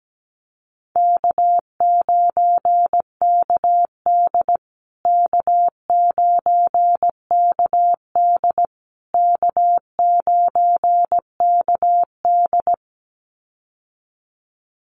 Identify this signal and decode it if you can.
Morse code, heard as a keyed tone, about 17 wpm, 700 Hz. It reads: K9KD K9KD K9KD
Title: USA Call Signs Repeated 3x 17wpm